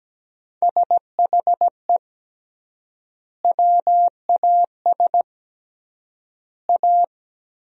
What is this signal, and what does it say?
Morse code, heard as a keyed tone, about 17 wpm, 700 Hz. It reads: SHE WAS A
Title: 3x Word-Spacing - Sentences from Top 400 Words - 17wpm